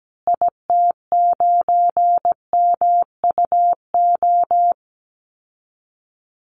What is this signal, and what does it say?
Morse code, heard as a keyed tone, about 17 wpm, 700 Hz. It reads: IT9MUO
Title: Call Signs Repeated 2x - 17wpm